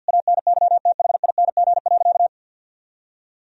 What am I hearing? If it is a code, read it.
Morse code, heard as a keyed tone, about 48 wpm, 700 Hz. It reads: ANYTHING.
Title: Sentences from Top 1000 Words - Review Part 2 - 50wpm